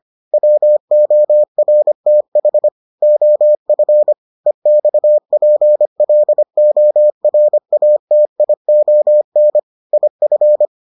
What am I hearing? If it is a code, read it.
Morse code, heard as a keyed tone, about 25 wpm, 600 Hz. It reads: WORTH OF EXPLORATION IF